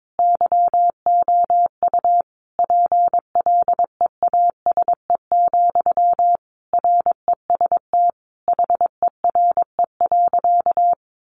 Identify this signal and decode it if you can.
Morse code, heard as a keyed tone, about 22 wpm, 700 Hz. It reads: YOU PLEASE, REST HERE.
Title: Sentences from Top 1500 Words 22wpm